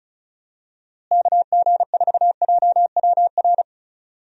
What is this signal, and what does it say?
Morse code, heard as a keyed tone, about 34 wpm, 700 Hz. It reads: KG4JWR
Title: USA Call Signs - 6 Characters - Repeated 2x - 35wpm